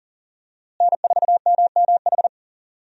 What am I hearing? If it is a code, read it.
Morse code, heard as a keyed tone, about 40 wpm, 700 Hz. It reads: N4MMH
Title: USA Call Signs - 5 Characters - Repeated 3x - 40wpm